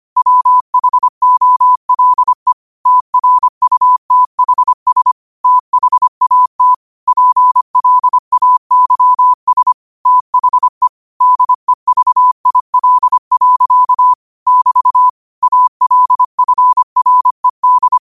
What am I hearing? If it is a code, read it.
Morse code, heard as a keyed tone, about 25 wpm, 1000 Hz. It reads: WHOLE TRUTHS THAT PLAYS THE DEVIL. <BT> ALFRED
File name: qotdfiles / 25wpm